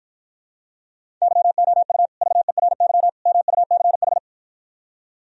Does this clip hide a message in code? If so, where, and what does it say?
Yes: Morse code from 1.2 to 4.2 s, heard as a keyed tone, about 53 wpm, 700 Hz. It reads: YOU VERY MUCH